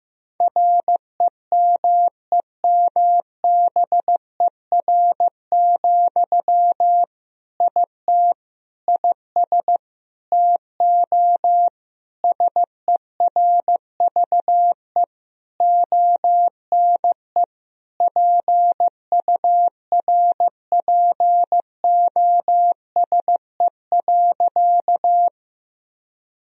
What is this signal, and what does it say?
Morse code, heard as a keyed tone, about 15 wpm, 700 Hz. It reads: REMEMBER, IT IS TO SERVE ONE PURPOSE.
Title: Sentences from Top 1000 Words - Review Part 2 - 15wpm